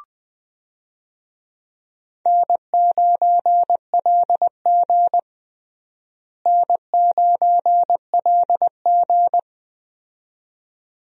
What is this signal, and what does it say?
Morse code, heard as a keyed tone, about 20 wpm, 700 Hz. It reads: N9LG N9LG
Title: USA Call Signs - 4 Characters - Repeated 2x - 20wpm